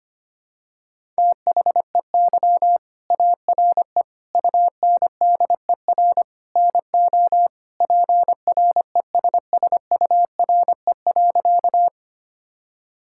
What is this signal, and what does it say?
Morse code, heard as a keyed tone, about 25 wpm, 700 Hz. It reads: THEY ARE UNDER NO PRESSURE.
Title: Sentences from Top 1000 Words - Review Part 6 - 25wpm